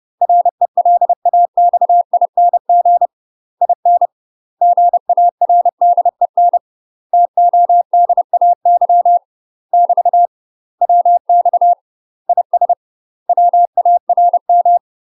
Morse code, heard as a keyed tone, about 30 wpm, 700 Hz. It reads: RELAXING IN GARDEN TODAY <BT> WX IS WARM
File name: UK MASTER QSOs TRAINING / 30WPM / UK MASTER QSOs 0011